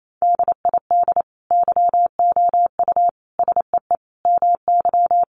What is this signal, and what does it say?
Morse code, heard as a keyed tone, about 28 wpm, 700 Hz. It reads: DID YOU SEE MY
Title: Sentences from Top 1500 Words 28wpm